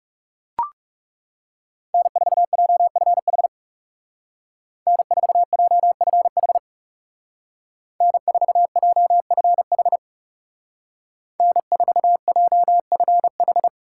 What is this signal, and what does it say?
Morse code, heard as a keyed tone, about 39 wpm, 700 Hz. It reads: N4JFH N4JFH N4JFH N4JFH